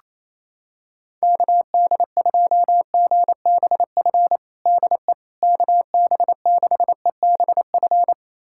Morse code, heard as a keyed tone, about 28 wpm, 700 Hz. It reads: KD2GBF DE KB6EBF
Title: USA Callsign DE Callsign 28wpm